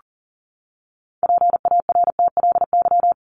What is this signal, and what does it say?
Morse code, heard as a keyed tone, about 40 wpm, 700 Hz. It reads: PARTLY
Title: LY Suffix - Top 100 Words with Base Words 40wpm